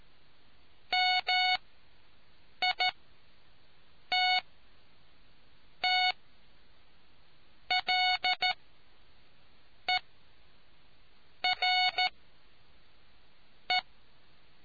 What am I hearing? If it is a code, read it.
Morse code, heard as a keyed tone, about 14 wpm, spaced as for 5 wpm, 740 Hz. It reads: MITTLERE